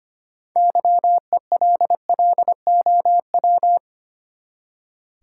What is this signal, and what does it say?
Morse code, heard as a keyed tone, about 25 wpm, 700 Hz. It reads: YELLOW
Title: Six Letter Words - 25wpm